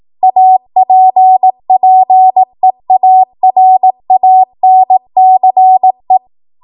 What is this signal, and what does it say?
Morse code, heard as a keyed tone, about 18 wpm, 750 Hz. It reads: APPEARANCE
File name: Archive / 18 WPM / 18WPM